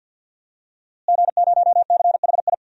Morse code, heard as a keyed tone, about 53 wpm, 700 Hz. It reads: G0CHI
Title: DXCC-100 Call Signs - 5 Characters - Repeated 2x - 50wpm